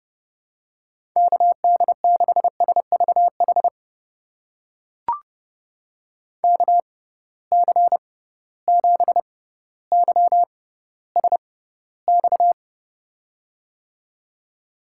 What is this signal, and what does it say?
Morse code, heard as a keyed tone, about 31 wpm, spaced as for 16 wpm, 700 Hz. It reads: KD6SVH KC7YSX
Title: USA Call Signs - 6 Characters - 6x Character Spacing - 30wpm